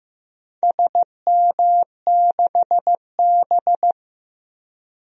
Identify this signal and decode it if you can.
Morse code, heard as a keyed tone, about 15 wpm, 700 Hz. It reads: SM6B
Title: DXCC-100 Call Signs - 4 Characters - 15wpm